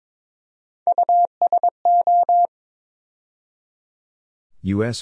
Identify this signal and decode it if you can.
Morse code, heard as a keyed tone, about 22 wpm, 700 Hz. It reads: USO
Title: USA Call Sign Postfixes - 3 Letters 22wpm